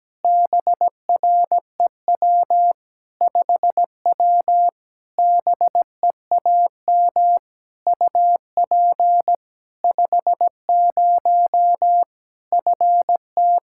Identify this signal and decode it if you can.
Morse code, heard as a keyed tone, about 17 wpm, 700 Hz. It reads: BREW 5W BEAM UP 50 FT
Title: Practice QSO - Exchange 2 - Info - 17wpm